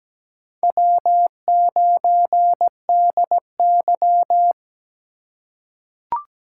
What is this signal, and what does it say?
Morse code, heard as a keyed tone, about 17 wpm, 700 Hz. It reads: W9DY
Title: K1USN SST QSO Practice - 17wpm